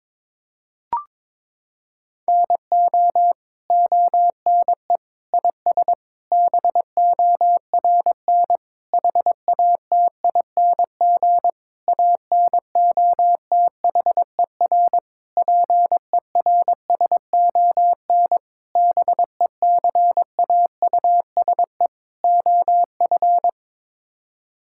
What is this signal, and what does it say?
Morse code, heard as a keyed tone, about 22 wpm, 700 Hz. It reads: NO ONE IS BORN HATING ANOTHER PERSON BECAUSE OF
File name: Sets of 9 Words - 22wpm